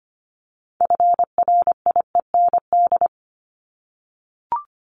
Morse code, heard as a keyed tone, about 25 wpm, 700 Hz. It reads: FRIEND
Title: Top 500 Words 25wpm